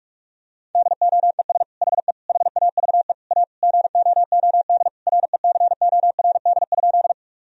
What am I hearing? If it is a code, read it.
Morse code, heard as a keyed tone, about 46 wpm, 700 Hz. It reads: DOES HE HAVE A GOOD RECORD?
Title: Sentences from Top 1000 Words - Review Part 3 - 45wpm